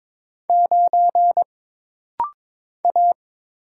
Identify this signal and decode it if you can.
Morse code, heard as a keyed tone, about 22 wpm, 700 Hz. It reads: 9 A
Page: This practice set is composed of Letters, Numbers, common punctuations, and common prosigns.